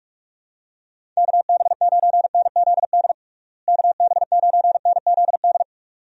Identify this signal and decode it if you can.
Morse code, heard as a keyed tone, about 43 wpm, 700 Hz. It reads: KB9NZD KB9NZD
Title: USA Call Signs - 6 Characters - Repeated 2x - 45wpm